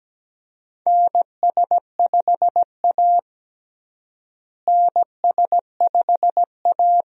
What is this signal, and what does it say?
Morse code, heard as a keyed tone, about 17 wpm, 700 Hz. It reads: NS5A NS5A
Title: USA Call Signs - 4 Characters - Repeated 3x - 17wpm